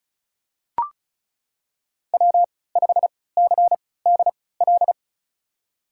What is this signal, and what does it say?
Morse code, heard as a keyed tone, about 36 wpm, spaced as for 20 wpm, 700 Hz. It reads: W5CDL
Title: USA Call Signs - 5 Characters - 3x Character Spacing - 35wpm